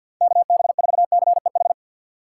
Morse code, heard as a keyed tone, about 50 wpm, 700 Hz. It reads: KB4CEH
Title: USA Call Signs - 6 Characters - 4x Character Spacing - 50wpm